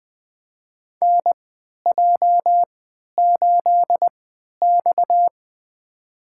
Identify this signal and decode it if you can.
Morse code, heard as a keyed tone, about 21 wpm, spaced as for 11 wpm, 700 Hz. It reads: NJ8X
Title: USA Call Signs - 4 Characters - 3x Character Spacing - 20wpm